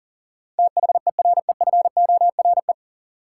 Morse code, heard as a keyed tone, about 40 wpm, 700 Hz. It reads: THEREFORE
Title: Sentences from Top 800 Words - New Word Drill - 40wpm